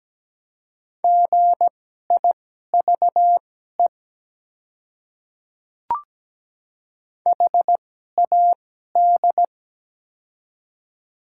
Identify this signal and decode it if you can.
Morse code, heard as a keyed tone, about 17 wpm, spaced as for 12 wpm, 700 Hz. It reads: GIVE HAD
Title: 2x Character Spacing - Top 200 Words - 17wpm